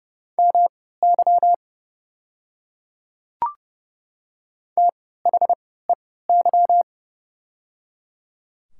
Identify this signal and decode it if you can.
Morse code, heard as a keyed tone, about 30 wpm, spaced as for 17 wpm, 700 Hz. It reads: MY THEY
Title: 3x Character Spacing - Top 100 Words - 30wpm